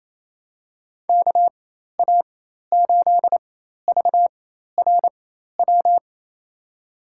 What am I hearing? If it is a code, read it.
Morse code, heard as a keyed tone, about 27 wpm, spaced as for 13 wpm, 700 Hz. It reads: KA8VRW
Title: USA Call Signs - 6 Characters - 4x Character Spacing - 28wpm